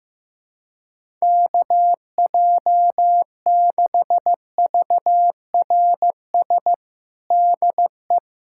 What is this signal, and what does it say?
Morse code, heard as a keyed tone, about 15 wpm, 700 Hz. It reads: KJ6VRS DE
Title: USA Callsign DE Callsign - 6 Characters - 15wpm